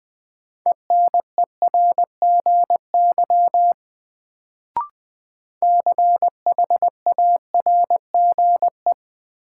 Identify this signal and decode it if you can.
Morse code, heard as a keyed tone, about 20 wpm, 700 Hz. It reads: ENERGY CHARGE
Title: Sentences from Top 900 Words - New Word Drill - 20wpm